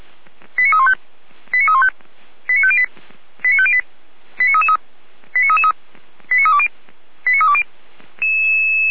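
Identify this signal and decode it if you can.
Five-tone selective call (ZVEI), heard as quick runs of five tones, tones of about 70 ms. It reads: 89315 89315 89508 89508 89333 89333 89320 89320
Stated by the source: ZVEI five-tone selective call received by SDR, logged by the fire brigade Feuerwehr Orlamünde.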